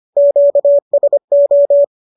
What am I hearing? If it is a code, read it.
Morse code, heard as a keyed tone, about 25 wpm, 590 Hz. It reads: QSO